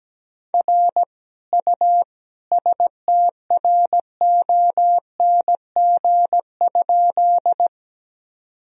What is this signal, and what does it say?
Morse code, heard as a keyed tone, about 17 wpm, 700 Hz. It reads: R U STRONG?
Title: Sentences from Top 500 Words and Abbreviations 17wpm